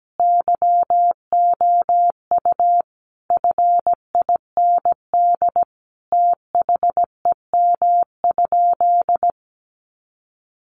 Morse code, heard as a keyed tone, about 17 wpm, 700 Hz. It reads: YOU FIND THEM?
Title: Sentences from Top 200 Words 17wpm